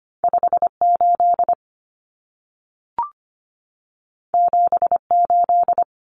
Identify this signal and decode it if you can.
Morse code, heard as a keyed tone, about 25 wpm, 700 Hz. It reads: 58 78
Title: Common Cut Numbers - 2-Digit 25wpm